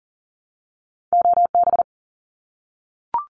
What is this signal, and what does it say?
Morse code, heard as a keyed tone, about 40 wpm, 700 Hz. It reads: OB
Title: Abbreviations 40wpm